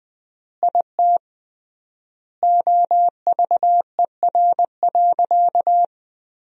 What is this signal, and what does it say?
Morse code, heard as a keyed tone, about 21 wpm, 700 Hz. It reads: IT OVER.
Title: 3x Word-Spacing - Sentences from Top 500 Words - 20wpm